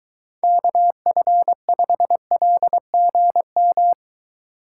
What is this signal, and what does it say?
Morse code, heard as a keyed tone, about 22 wpm, 700 Hz. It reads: KF5LGM